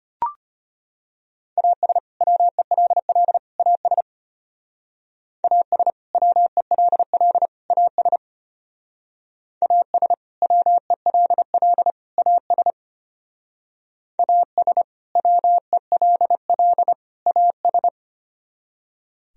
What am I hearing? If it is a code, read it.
Morse code, heard as a keyed tone, about 33 wpm, 700 Hz. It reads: AS WELL AS AS WELL AS AS WELL AS AS WELL AS